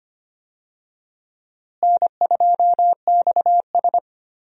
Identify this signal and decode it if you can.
Morse code, heard as a keyed tone, about 25 wpm, 700 Hz. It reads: N2XS
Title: USA Call Signs - 4 Characters - Repeated 2x - 25wpm